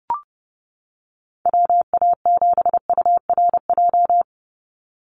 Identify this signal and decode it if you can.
Morse code, heard as a keyed tone, about 29 wpm, 700 Hz. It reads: WA7URJ
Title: USA Call Signs 30wpm